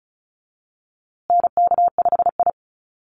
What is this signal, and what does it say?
Morse code, heard as a keyed tone, about 34 wpm, 700 Hz. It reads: NK5I
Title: USA Call Signs Repeated 2x 35wpm